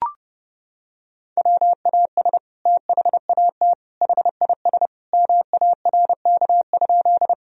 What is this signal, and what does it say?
Morse code, heard as a keyed tone, about 30 wpm, 700 Hz. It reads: WAS THAT HIS MARK?
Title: Sentences from Top 1000 Words - Review Part 1 - 30wpm